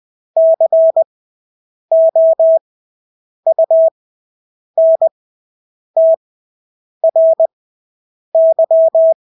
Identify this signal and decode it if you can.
Morse code, heard as a keyed tone, about 20 wpm, spaced as for 8 wpm, 650 Hz. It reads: COUNTRY
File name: Engelske ord 20/8 wpm